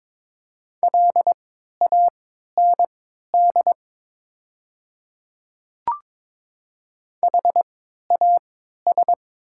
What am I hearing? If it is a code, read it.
Morse code, heard as a keyed tone, about 22 wpm, spaced as for 12 wpm, 700 Hz. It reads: LAND HAS
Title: 3x Character Spacing - Top 300 Words - 22wpm